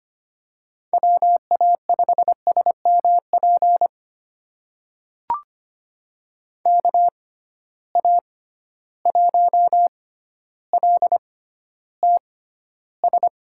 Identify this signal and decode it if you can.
Morse code, heard as a keyed tone, about 25 wpm, spaced as for 13 wpm, 700 Hz. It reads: WA5SMP KA1LTS
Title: USA Call Signs - 6 Characters - 6x Character Spacing - 25wpm